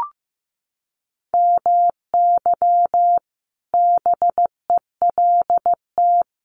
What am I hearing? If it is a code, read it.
Morse code, heard as a keyed tone, about 14 wpm, 700 Hz. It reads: MY BELT
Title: Sentences from Top 1400 Words 15wpm